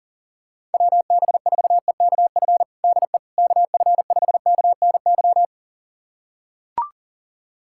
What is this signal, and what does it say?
Morse code, heard as a keyed tone, about 40 wpm, 700 Hz. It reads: WB4EKF DE KF5KNY
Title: USA Callsign DE Callsign - 6 Characters - 40wpm